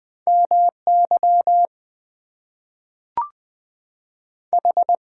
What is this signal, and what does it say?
Morse code, heard as a keyed tone, about 20 wpm, 700 Hz. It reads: MY H
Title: Two Letter Words - 20wpm